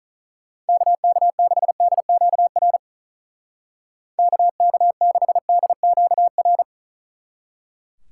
Morse code, heard as a keyed tone, about 46 wpm, 700 Hz. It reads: KK6DQR KK6DQR